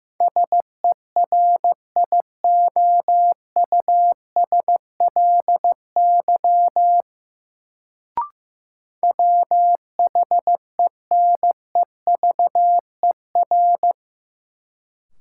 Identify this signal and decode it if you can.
Morse code, heard as a keyed tone, about 15 wpm, 700 Hz. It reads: SERIOUSLY WHENEVER
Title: Sentences from Top 1200 Words - New Word Drill - 15wpm